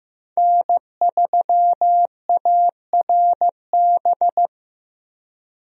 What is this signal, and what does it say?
Morse code, heard as a keyed tone, about 16 wpm, 700 Hz. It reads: N3ARB
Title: USA Call Signs - 5 Characters - Repeated 2x - 15wpm